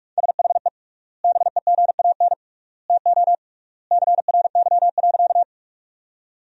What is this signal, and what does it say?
Morse code, heard as a keyed tone, about 46 wpm, 700 Hz. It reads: SHE BEGAN TO CRY.
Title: 3x Word-Spacing - Sentences from Top 500 Words - 45wpm